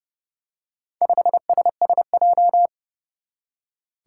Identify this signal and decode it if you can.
Morse code, heard as a keyed tone, about 30 wpm, 700 Hz. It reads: 5SSJ
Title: Four Challenging Letter-Number Combinations 30wpm